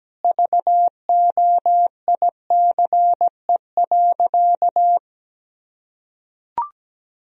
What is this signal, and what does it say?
Morse code, heard as a keyed tone, about 17 wpm, 700 Hz. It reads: VOICE.
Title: Sentences from Top 1000 Words - Review Part 1 - 17wpm